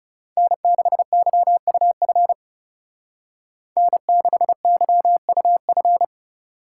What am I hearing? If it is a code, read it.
Morse code, heard as a keyed tone, about 35 wpm, 700 Hz. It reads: N6YUF N6YUF
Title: Speed Racing - USA 1x3 Call Signs 30wpm